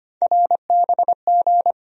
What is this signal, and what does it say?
Morse code, heard as a keyed tone, about 25 wpm, 700 Hz. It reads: RBG
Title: USA Call Sign Postfixes - 3 Letters 25wpm